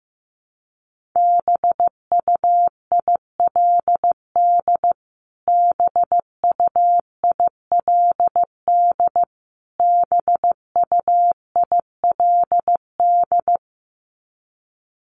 Morse code, heard as a keyed tone, about 15 wpm, 700 Hz. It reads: BUILD BUILD BUILD
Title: Top 500 Words Repeated 15wpm